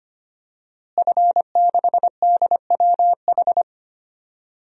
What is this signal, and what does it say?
Morse code, heard as a keyed tone, about 25 wpm, 700 Hz. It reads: F6DWH
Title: DXCC-100 Call Signs - 5 Characters - Repeated 2x - 25wpm